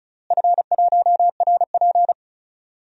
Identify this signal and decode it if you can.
Morse code, heard as a keyed tone, about 36 wpm, 700 Hz. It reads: F1RP
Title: DXCC-100 Call Signs - 4 Characters - Repeated 3x - 35wpm